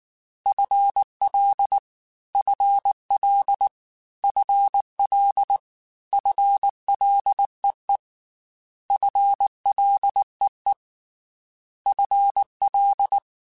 Morse code, heard as a keyed tone, about 19 wpm, 800 Hz. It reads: FL FL FL FLEE FLEE FL
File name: FL / FL mp3s / FL Prefixes 3x2 Twice @18